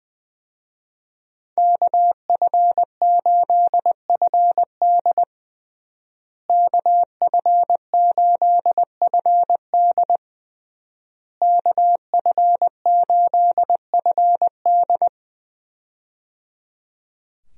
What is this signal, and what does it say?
Morse code, heard as a keyed tone, about 20 wpm, 700 Hz. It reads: KF8FD KF8FD KF8FD
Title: USA Call Signs - 5 Characters - Repeated 3x - 20wpm